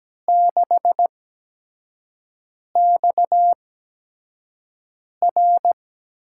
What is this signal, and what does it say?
Morse code, heard as a keyed tone, about 17 wpm, spaced as for 5 wpm, 700 Hz. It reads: 6XR
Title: USA Call Signs - 8x Character Spacing - 17wpm